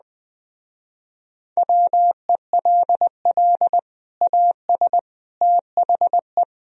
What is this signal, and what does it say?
Morse code, heard as a keyed tone, about 21 wpm, 700 Hz. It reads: WELL AS THE
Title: Speed-Racing - Sets of 3 Words 17wpm